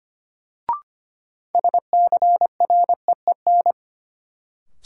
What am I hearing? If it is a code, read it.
Morse code, heard as a keyed tone, about 25 wpm, 700 Hz. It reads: SCREEN
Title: Sentences from Top 1000 Words - Review Part 6 - New Word Drill - 25wpm